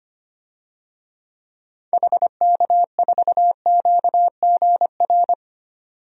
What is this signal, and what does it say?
Morse code, heard as a keyed tone, about 25 wpm, 700 Hz. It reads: HK4QGR
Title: DXCC-100 Call Signs - 6 Characters - 25wpm